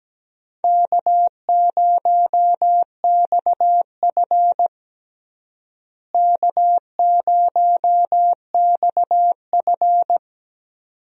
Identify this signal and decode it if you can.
Morse code, heard as a keyed tone, about 17 wpm, 700 Hz. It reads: K0XF K0XF
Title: USA Call Signs - 4 Characters - Repeated 2x - 17wpm